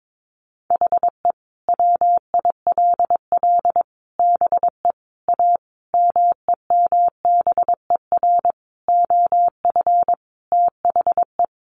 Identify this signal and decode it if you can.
Morse code, heard as a keyed tone, about 22 wpm, 700 Hz. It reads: HE WILL BE A MEMBER OF THE
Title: Sentences from Top 1100 Words 22wpm